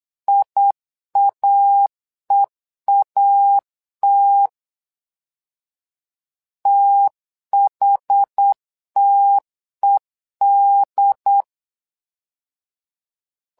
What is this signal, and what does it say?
Morse code, heard as a keyed tone, about 8 wpm, 800 Hz. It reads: IAEAT THTED
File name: LEZIONI / 2 UHDN / 2 UHD 8wpm 2258